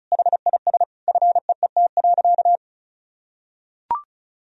Morse code, heard as a keyed tone, about 34 wpm, 700 Hz. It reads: HIS FEET.
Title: Sentences from Top 1000 Words - Review Part 1 - 35wpm